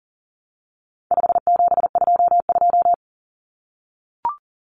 Morse code, heard as a keyed tone, about 40 wpm, 700 Hz. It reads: 5722
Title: Common Cut Numbers - 4-Digit 40wpm